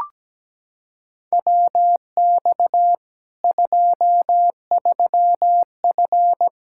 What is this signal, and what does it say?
Morse code, heard as a keyed tone, about 17 wpm, 700 Hz. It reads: WX 23F
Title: K1USN SST QSO Practice 17wpm